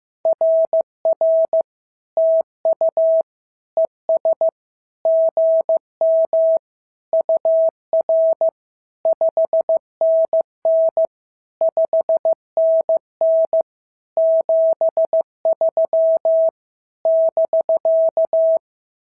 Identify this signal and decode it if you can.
Morse code, heard as a keyed tone, about 14 wpm, 650 Hz. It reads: RR TU ES GM UR 5NN 5NN 73 <BK>
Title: SOTA QSO Practice 15wpm